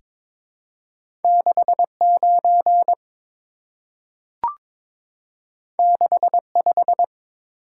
Morse code, heard as a keyed tone, about 22 wpm, 700 Hz. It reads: 69 65
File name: practice / 22wpm / Two Letter-Number Combinations - 22wpm